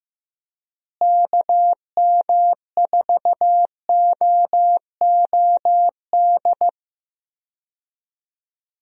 Morse code, heard as a keyed tone, about 15 wpm, 700 Hz. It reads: KM4OOD
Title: USA Call Signs - 6 Characters - Repeated 3x - 15wpm